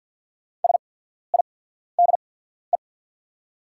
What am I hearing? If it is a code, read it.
Morse code, heard as a keyed tone, about 50 wpm, spaced as for 14 wpm, 690 Hz. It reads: SIDE
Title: 8x Character Spacing - Top 100 Words - 50wpm